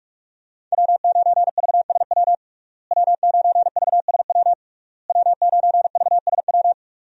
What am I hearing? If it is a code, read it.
Morse code, heard as a keyed tone, about 41 wpm, 700 Hz. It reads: W9VSW W9VSW W9VSW
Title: USA Call Signs - 5 Characters - Repeated 3x - 45wpm